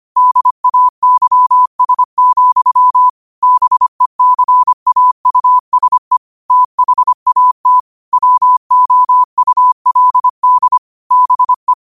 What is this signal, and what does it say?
Morse code, heard as a keyed tone, about 25 wpm, 1000 Hz. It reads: DAYS, BECAUSE THAT WOULD BE
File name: qotdfiles / 25wpm